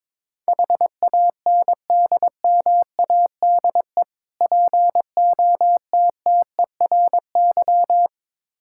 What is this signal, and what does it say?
Morse code, heard as a keyed tone, about 22 wpm, 700 Hz. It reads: HANDMADE POTTERY
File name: Difficult Words with Hints - 22wpm